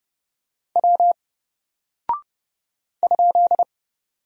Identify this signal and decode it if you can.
Morse code, heard as a keyed tone, about 30 wpm, 700 Hz. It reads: W ?
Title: Single Character (Letter:Number:Punctuation:Prosign) 30wpm